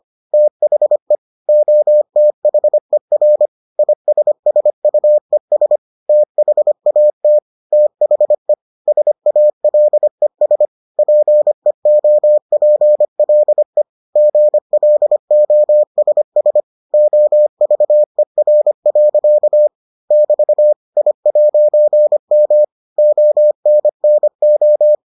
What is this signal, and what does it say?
Morse code, heard as a keyed tone, about 25 wpm, 600 Hz. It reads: THE OTHER ISSUES THAT THE SALES PEOPLE GLOSS OVER. <BT> I'M ONNO